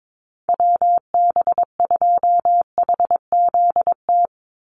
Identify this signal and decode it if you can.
Morse code, heard as a keyed tone, about 22 wpm, 700 Hz. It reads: WB2HZT
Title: USA Call Signs 22wpm